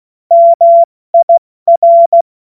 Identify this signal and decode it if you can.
Morse code, heard as a keyed tone, about 16 wpm, 680 Hz. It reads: MIR
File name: Morsekurs / Witze / Witz 29. Juni 2021, 70 F80